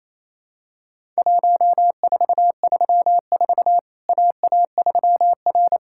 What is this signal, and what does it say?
Morse code, heard as a keyed tone, about 28 wpm, 700 Hz. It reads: 1434 AA3R
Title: Worked All Europe DX Contest - QTC Exchange - 28wpm